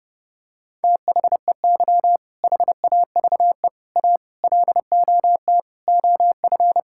Morse code, heard as a keyed tone, about 30 wpm, 700 Hz. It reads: THEY HAVE A LOT OF
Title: Sentences from Top 1000 Words - Review Part 5 - 30wpm